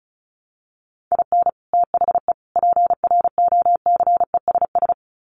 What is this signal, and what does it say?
Morse code, heard as a keyed tone, about 34 wpm, 700 Hz. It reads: IN THE PROCESS
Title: Sets of 3 Words 35wpm